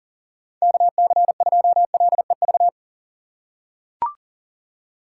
Morse code, heard as a keyed tone, about 38 wpm, 700 Hz. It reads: KC2LEV
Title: USA Call Signs - 6 Characters - 6x Character Spacing - 40wpm